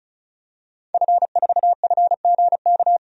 Morse code, heard as a keyed tone, about 34 wpm, 700 Hz. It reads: F4FGK
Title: DXCC-100 Call Signs - 5 Characters - 35wpm